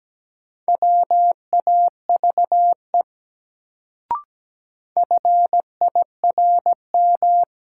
Morse code, heard as a keyed tone, about 17 wpm, 700 Hz. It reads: WAVE FIRM
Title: Sentences from Top 1400 Words - New Word Drill Alt - 17wpm